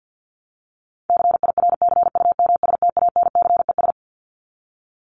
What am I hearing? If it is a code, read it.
Morse code, heard as a keyed tone, about 48 wpm, 700 Hz. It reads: CIRCUMSTANCES
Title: ES:S Suffix - Top 100 Words with Base Words 50wpm